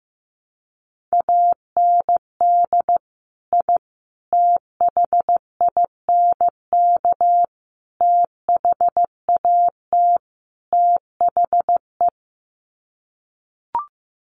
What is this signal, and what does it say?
Morse code, heard as a keyed tone, about 15 wpm, 700 Hz. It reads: AND I THINK THAT THE
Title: Sets of 5 Words 15wpm